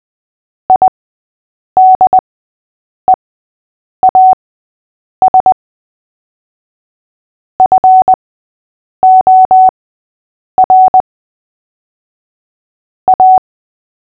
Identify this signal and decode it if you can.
Morse code, heard as a keyed tone, about 20 wpm, spaced as for 8 wpm, 750 Hz. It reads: IDEAS FOR A
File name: cw / fists / Short-Story-6-8wpm